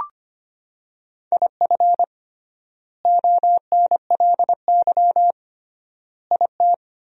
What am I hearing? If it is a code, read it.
Morse code, heard as a keyed tone, about 25 wpm, 700 Hz. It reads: IF ONLY IT
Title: 3x Word-Spacing - Sentences from Top 500 Words - 25wpm